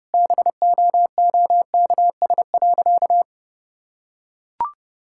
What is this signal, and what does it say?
Morse code, heard as a keyed tone, about 32 wpm, 700 Hz. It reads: BOOKS.
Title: Sentences from Top 1000 Words - Review Part 6 - 30wpm